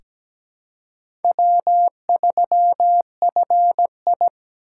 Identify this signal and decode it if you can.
Morse code, heard as a keyed tone, about 17 wpm, 700 Hz. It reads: W3FI
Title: USA Callsign DE Callsign - 4 Characters - 17wpm